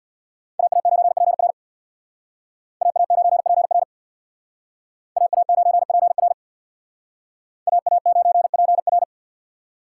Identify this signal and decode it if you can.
Morse code, heard as a keyed tone, about 57 wpm, 690 Hz. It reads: AA9PR AA9PR AA9PR AA9PR
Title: Speed Racing - USA 2x2 Call Signs 50wpm